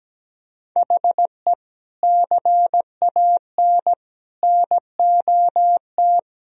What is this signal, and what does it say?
Morse code, heard as a keyed tone, about 17 wpm, 700 Hz. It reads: HE CAN NOT
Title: Sentences from Top 1000 Words - Review Part 1 - 17wpm